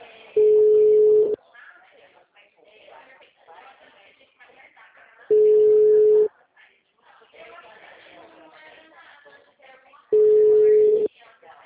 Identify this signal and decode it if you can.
Ringback tone (Europe, 425 Hz), with speech.